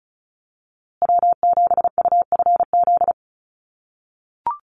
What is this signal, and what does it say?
Morse code, heard as a keyed tone, about 34 wpm, 700 Hz. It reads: W7UFZ
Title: USA Call Signs Repeated 3x 35wpm